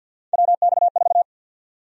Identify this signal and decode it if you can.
Morse code, heard as a keyed tone, about 51 wpm, 700 Hz. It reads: WX4
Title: USA Call Sign Prefixes - 50wpm